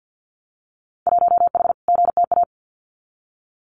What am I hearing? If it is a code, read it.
Morse code, heard as a keyed tone, about 50 wpm, 700 Hz. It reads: 1H GTA
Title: Winter Field Day Exchanges 50wpm